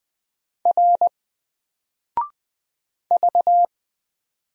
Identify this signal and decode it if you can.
Morse code, heard as a keyed tone, about 20 wpm, 700 Hz. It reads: R V
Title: Single Letter-Number Alt 20wpm